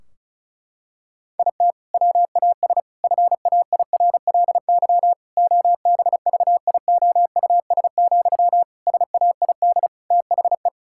Morse code, heard as a keyed tone, about 36 wpm, 700 Hz. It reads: IT WAS FAIRLY OBVIOUS, SAID THE
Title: Sentences from Top 1600 Words 35wpm